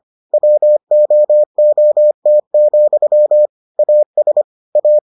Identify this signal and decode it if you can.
Morse code, heard as a keyed tone, about 25 wpm, 600 Hz. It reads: WOOT, AS A